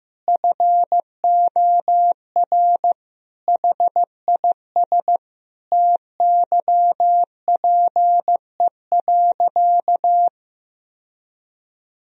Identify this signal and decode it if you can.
Morse code, heard as a keyed tone, about 15 wpm, 700 Hz. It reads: FOR HIS TYPE.
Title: Sentences from Top 1000 Words - Review Part 3 - 15wpm